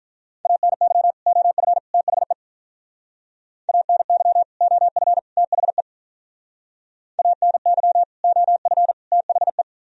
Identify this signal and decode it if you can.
Morse code, heard as a keyed tone, about 43 wpm, 700 Hz. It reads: ANY OF THE ANY OF THE ANY OF THE
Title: Speed-Racing - Sets of 3 Words 35wpm